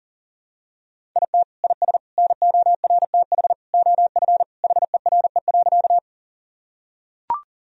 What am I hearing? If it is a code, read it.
Morse code, heard as a keyed tone, about 40 wpm, 700 Hz. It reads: IT IS NORTH OF HERE.
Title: Sentences from Top 1000 Words - Review Part 1 - 40wpm